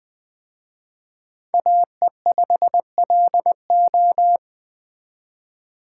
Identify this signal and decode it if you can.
Morse code, heard as a keyed tone, about 19 wpm, 700 Hz. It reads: AE5LO